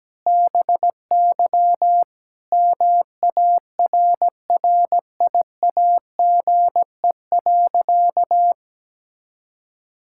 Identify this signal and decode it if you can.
Morse code, heard as a keyed tone, about 17 wpm, 700 Hz. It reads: BY MARRIAGE.
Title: Sentences from Top 1000 Words - Review Part 5 - 17wpm